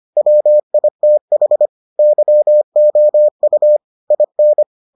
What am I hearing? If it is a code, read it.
Morse code, heard as a keyed tone, about 25 wpm, 600 Hz. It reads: WITH YOU IN